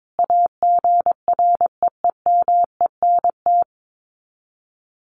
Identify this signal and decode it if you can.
Morse code, heard as a keyed tone, about 22 wpm, 700 Hz. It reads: AGREEMENT
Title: Sets of 5 Words Encore 22wpm